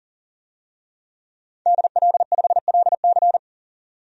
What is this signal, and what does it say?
Morse code, heard as a keyed tone, about 40 wpm, 700 Hz. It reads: DL5LC
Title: DXCC-100 Call Signs - 5 Characters - 40wpm